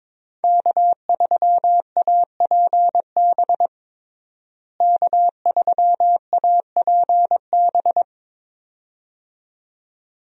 Morse code, heard as a keyed tone, about 22 wpm, 700 Hz. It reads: K3APB K3APB
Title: USA Call Signs - 5 Characters - Repeated 2x - 22wpm